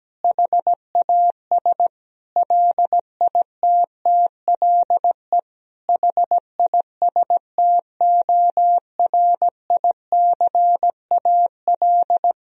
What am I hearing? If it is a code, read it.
Morse code, heard as a keyed tone, about 17 wpm, 700 Hz. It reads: HAS LITTLE HISTORICAL
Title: Sentences from Top 1000 Words - Review Part 4 - 17wpm